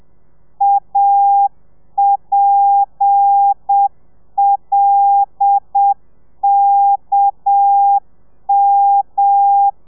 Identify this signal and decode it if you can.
Morse code, heard as a keyed tone, about 7 wpm, 800 Hz. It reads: APLKM
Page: Lezione 1: audio a 30 CPM